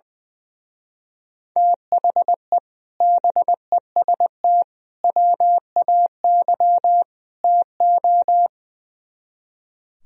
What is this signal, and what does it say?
Morse code, heard as a keyed tone, about 21 wpm, 700 Hz. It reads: THE BEST WAY TO
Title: Speed-Racing - Sets of 4 Words 20wpm